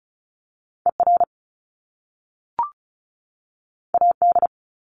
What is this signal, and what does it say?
Morse code, heard as a keyed tone, about 36 wpm, 700 Hz. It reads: ER AD
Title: Top Two Letter Combinations in Words 35wpm